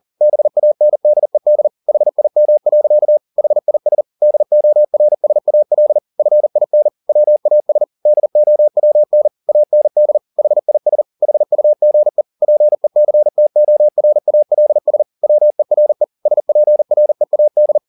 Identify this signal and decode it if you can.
Morse code, heard as a keyed tone, about 40 wpm, 600 Hz. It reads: BANDED HIM. HIS DORSAL FIN WAS DOWN AND HIS HUGE PECTORALS WERE SPREAD
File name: the old man and the sea 2 40wpm